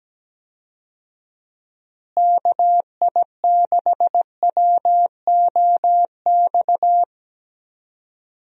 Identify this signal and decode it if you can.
Morse code, heard as a keyed tone, about 17 wpm, 700 Hz. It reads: KI6WOX
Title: USA Call Signs - 6 Characters - Repeated 2x - 17wpm